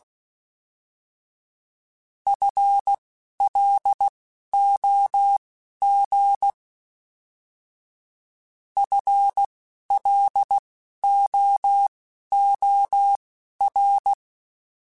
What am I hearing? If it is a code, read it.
Morse code, heard as a keyed tone, about 16 wpm, spaced as for 11 wpm, 800 Hz. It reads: FLOG FLOOR